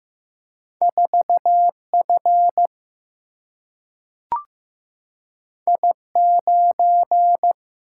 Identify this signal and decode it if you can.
Morse code, heard as a keyed tone, about 15 wpm, 700 Hz. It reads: 4 F I 9
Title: Single-Letter Single-Number 15wpm